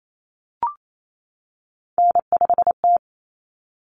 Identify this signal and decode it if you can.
Morse code, heard as a keyed tone, about 28 wpm, 700 Hz. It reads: N5T
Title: Common Cut Numbers - 3-Digit 28wpm